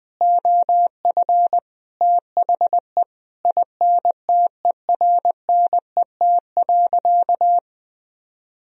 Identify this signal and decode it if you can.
Morse code, heard as a keyed tone, about 21 wpm, 700 Hz. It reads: OF THE INTERNET.
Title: Sentences from Top 1000 Words - Review Part 4 - 20wpm